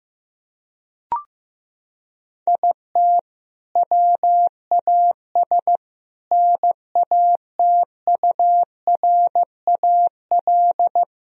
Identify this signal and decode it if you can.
Morse code, heard as a keyed tone, about 15 wpm, 700 Hz. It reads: IT WAS NATURAL
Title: Sentences from Top 1000 Words - Review Part 4 - 15wpm